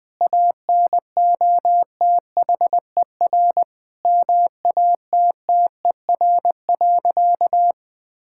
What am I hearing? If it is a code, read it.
Morse code, heard as a keyed tone, about 20 wpm, 700 Hz. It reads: ANOTHER MATTER.
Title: Sentences from Top 1000 Words - Review Part 5 - 20wpm